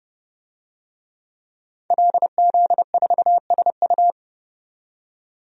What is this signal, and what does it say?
Morse code, heard as a keyed tone, about 29 wpm, 700 Hz. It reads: LZ4SU
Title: DXCC-100 Call Signs - 5 Characters - 30wpm